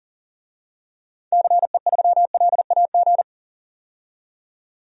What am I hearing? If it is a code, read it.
Morse code, heard as a keyed tone, about 40 wpm, 700 Hz. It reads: CE3LAG
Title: DXCC-100 Call Signs - 6 Characters - 40wpm